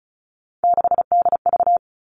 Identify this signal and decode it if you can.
Morse code, heard as a keyed tone, about 34 wpm, 700 Hz. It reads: 6DV
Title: All Cut Numbers - 3-Digit 35wpm